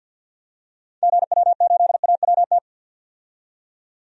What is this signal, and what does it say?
Morse code, heard as a keyed tone, about 52 wpm, 700 Hz. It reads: GW8AWT
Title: DXCC-100 Call Signs - 6 Characters - 50wpm